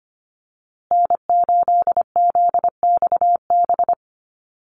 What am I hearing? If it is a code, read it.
Morse code, heard as a keyed tone, about 25 wpm, 700 Hz. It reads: N8ZXB
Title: USA Call Signs Repeated 3x 25wpm